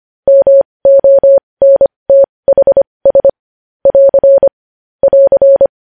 Morse code, heard as a keyed tone, about 25 wpm, spaced as for 20 wpm, 550 Hz. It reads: MONTHS <AR> <AR>
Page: news-headlines-in-morse-code-at-20-wpm